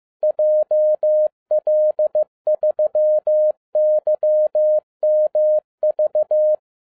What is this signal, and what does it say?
Morse code, heard as a keyed tone, about 15 wpm, 610 Hz. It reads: JL3YMV
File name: 2011-10-mp3 / n12-JL3YMV 15wpm